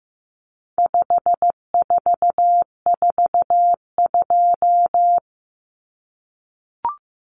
Morse code, heard as a keyed tone, about 15 wpm, 700 Hz. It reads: 5442
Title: Numbers - 4-Digit 15wpm